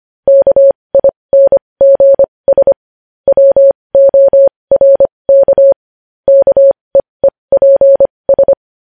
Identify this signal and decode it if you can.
Morse code, heard as a keyed tone, about 25 wpm, spaced as for 20 wpm, 550 Hz. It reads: KINGS WORK KEEPS
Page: News Headlines in Morse Code at 20 WPM